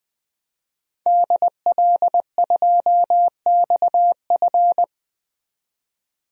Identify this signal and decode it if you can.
Morse code, heard as a keyed tone, about 20 wpm, 700 Hz. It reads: DL2XF
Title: DXCC-100 Call Signs - 5 Characters - 20wpm